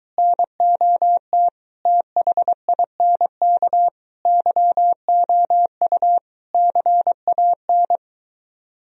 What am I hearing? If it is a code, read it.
Morse code, heard as a keyed tone, about 23 wpm, 700 Hz. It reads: NOT THINK YOU CAN